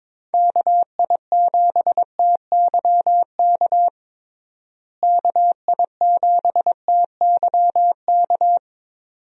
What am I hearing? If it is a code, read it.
Morse code, heard as a keyed tone, about 22 wpm, 700 Hz. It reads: KI7TYK KI7TYK
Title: USA Call Signs - 6 Characters - Repeated 2x - 22wpm